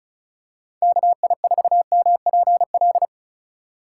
Morse code, heard as a keyed tone, about 36 wpm, 700 Hz. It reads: KI4MPL
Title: USA Call Signs - 6 Characters - 3x Character Spacing - 35wpm